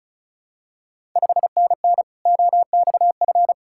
Morse code, heard as a keyed tone, about 36 wpm, 700 Hz. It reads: 5NN OXF
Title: Ontario QSO Party Exchange 35wpm